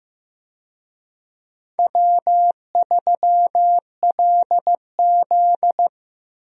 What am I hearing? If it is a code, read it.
Morse code, heard as a keyed tone, about 15 wpm, 700 Hz. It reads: W3LZ
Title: USA Call Signs - 4 Characters - Repeated 3x - 15wpm